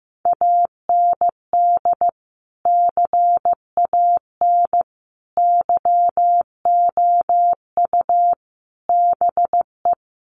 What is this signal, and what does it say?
Morse code, heard as a keyed tone, about 15 wpm, 700 Hz. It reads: AND CAN YOU BE
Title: Sentences from Top 1100 Words 15wpm